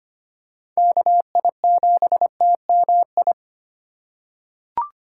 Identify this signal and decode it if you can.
Morse code, heard as a keyed tone, about 25 wpm, 700 Hz. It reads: KI7TMI
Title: USA Call Signs - 2x Character Spacing - 25wpm